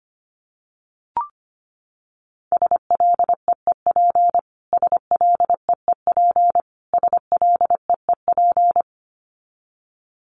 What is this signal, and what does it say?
Morse code, heard as a keyed tone, about 25 wpm, 700 Hz. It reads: SLEEP SLEEP SLEEP
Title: Top 500 Words Repeated 25wpm